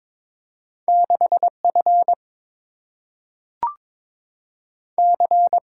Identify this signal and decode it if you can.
Morse code, heard as a keyed tone, about 22 wpm, 690 Hz. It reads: 6F C
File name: practice / 22wpm / Two Letter-Number Combinations - 22wpm